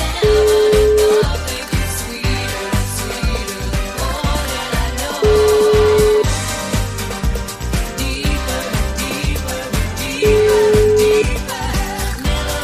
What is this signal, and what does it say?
Ringback tone (Europe, 425 Hz), with noise.